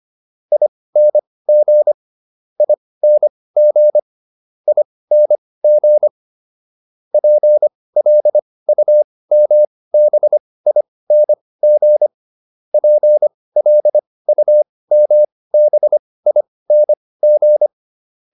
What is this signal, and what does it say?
Morse code, heard as a keyed tone, about 25 wpm, spaced as for 18 wpm, 600 Hz. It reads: ING ING ING PLUMBING PLUMBING